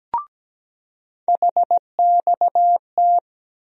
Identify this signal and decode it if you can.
Morse code, heard as a keyed tone, about 17 wpm, 700 Hz. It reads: HXT
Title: USA Call Sign Postfixes - 3 Letters 17wpm